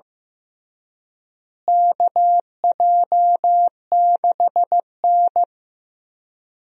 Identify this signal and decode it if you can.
Morse code, heard as a keyed tone, about 14 wpm, 700 Hz. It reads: KJ6N
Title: USA Call Signs - 4 Characters - Repeated 3x - 15wpm